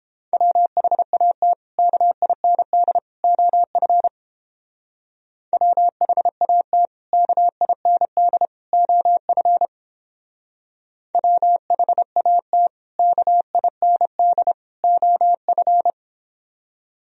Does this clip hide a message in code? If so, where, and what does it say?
Yes: Morse code from 0.3 to 15.9 s, heard as a keyed tone, about 31 wpm, 700 Hz. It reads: WHAT KIND OF WHAT KIND OF WHAT KIND OF